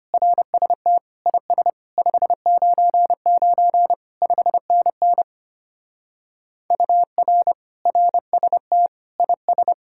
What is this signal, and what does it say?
Morse code, heard as a keyed tone, about 33 wpm, 700 Hz. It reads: RST IS 599 5NN UR RST IS